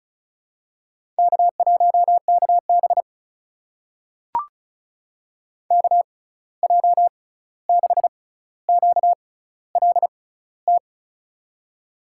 Morse code, heard as a keyed tone, about 36 wpm, spaced as for 16 wpm, 700 Hz. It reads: K1KB KJ6QLT
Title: USA Call Signs - 6x Character Spacing - 35wpm